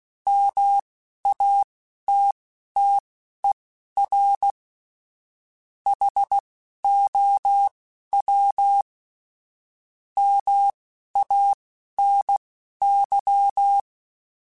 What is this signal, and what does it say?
Morse code, heard as a keyed tone, about 16 wpm, spaced as for 11 wpm, 800 Hz. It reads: MATTER HOW MANY
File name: Technical Subjects / Technical Subjects-mp3s / OR Gate @10